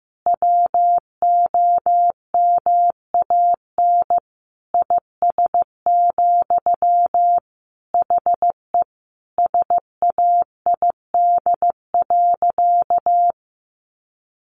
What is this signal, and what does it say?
Morse code, heard as a keyed tone, about 15 wpm, 700 Hz. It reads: WOMAN IS, HE SAID.
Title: Sentences from Top 1200 Words 15wpm